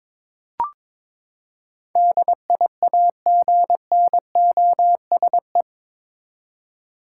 Morse code, heard as a keyed tone, about 21 wpm, 700 Hz. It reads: DIAGNOSE
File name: practice / 22wpm / Eight Letter Words - 22wpm